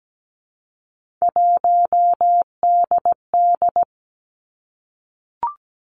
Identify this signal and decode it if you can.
Morse code, heard as a keyed tone, about 17 wpm, 700 Hz. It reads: 1DD
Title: Three Challenging Letter-Number Combinations 17wpm